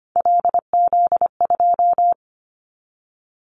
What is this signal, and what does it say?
Morse code, heard as a keyed tone, about 25 wpm, 700 Hz. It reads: LZ2
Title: Callsign Prefixes for Top 100 DXCC - 25wpm